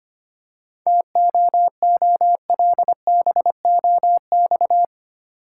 Morse code, heard as a keyed tone, about 27 wpm, 700 Hz. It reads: TOOLBOX
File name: practice / 25wpm / Difficult Words - Speed-Racing with Hints - 25wpm